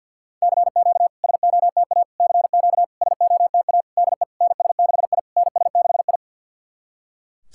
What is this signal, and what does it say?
Morse code, heard as a keyed tone, about 51 wpm, 690 Hz. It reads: CQ SOTA CQ SOTA DE NS6I NS6I
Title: SOTA QSO Practice 50wpm